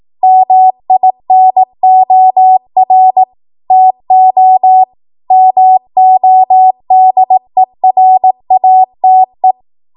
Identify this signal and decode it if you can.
Morse code, heard as a keyed tone, about 18 wpm, 750 Hz. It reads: MINOR TO MODERATE
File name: Morse / 18WPM